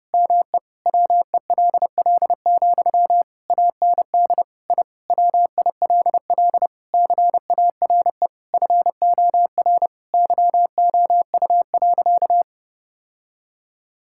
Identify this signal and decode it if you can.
Morse code, heard as a keyed tone, about 30 wpm, 700 Hz. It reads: ME WELL, AND I WILL CARE FOR YOU.
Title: Sentences from Top 1000 Words - Review Part 6 - 30wpm